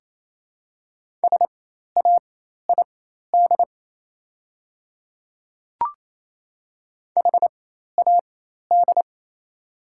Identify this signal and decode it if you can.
Morse code, heard as a keyed tone, about 28 wpm, spaced as for 13 wpm, 700 Hz. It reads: SAID HAD
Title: 4x Character Spacing - Top 100 Words - 28wpm